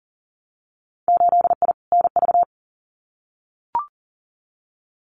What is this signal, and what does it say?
Morse code, heard as a keyed tone, about 40 wpm, 700 Hz. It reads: 8I NV
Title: Winter Field Day Exchanges Encore 40wpm